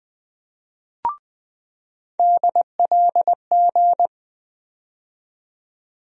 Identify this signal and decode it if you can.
Morse code, heard as a keyed tone, about 20 wpm, 700 Hz. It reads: DLG
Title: USA Call Sign Postfixes - 3 Letters 20wpm